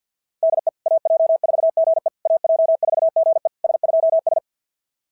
Morse code, heard as a keyed tone, about 48 wpm, 690 Hz. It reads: DE AJ4GE AJ4GE S2S
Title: SOTA QSO Practice 50wpm